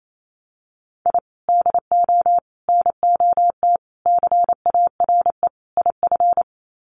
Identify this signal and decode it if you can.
Morse code, heard as a keyed tone, about 28 wpm, 700 Hz. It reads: I DO NOT CARE IF
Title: Sets of 5 Words 28wpm